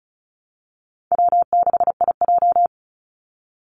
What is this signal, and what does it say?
Morse code, heard as a keyed tone, about 34 wpm, 700 Hz. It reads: W6IJ
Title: Call Signs - 35wpm